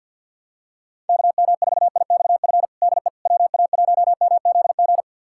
Morse code, heard as a keyed tone, about 52 wpm, 700 Hz. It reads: KM4IXF DE WA1MZG
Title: USA Callsign DE Callsign 50wpm